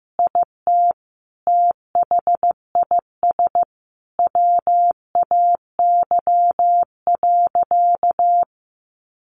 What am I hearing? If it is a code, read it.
Morse code, heard as a keyed tone, about 15 wpm, 700 Hz. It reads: IT THIS WAY.
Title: Sentences from Top 100 Words 15wpm